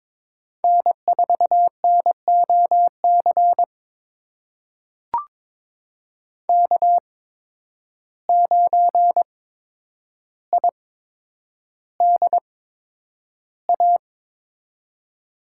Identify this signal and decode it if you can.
Morse code, heard as a keyed tone, about 22 wpm, spaced as for 9 wpm, 700 Hz. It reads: N4NOC K9IDA
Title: USA Call Signs - 5 Characters - 8x Character Spacing - 22wpm